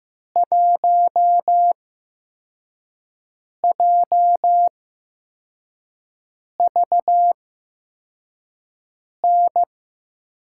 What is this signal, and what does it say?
Morse code, heard as a keyed tone, about 15 wpm, spaced as for 4 wpm, 700 Hz. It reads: 1JVN
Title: USA Call Signs - 6 Characters - 8x Character Spacing - 15wpm